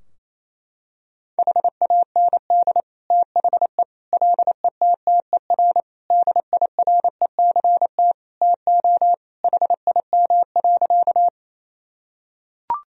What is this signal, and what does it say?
Morse code, heard as a keyed tone, about 28 wpm, 700 Hz. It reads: HAND THE LETTER DIRECT TO HIM.
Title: Sentences from Top 1000 Words - Review Part 2 - 28wpm